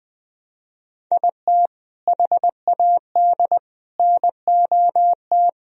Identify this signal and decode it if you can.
Morse code, heard as a keyed tone, about 21 wpm, 700 Hz. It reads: IT HAD NOT
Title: Sentences from Top 1000 Words - Review Part 1 - 20wpm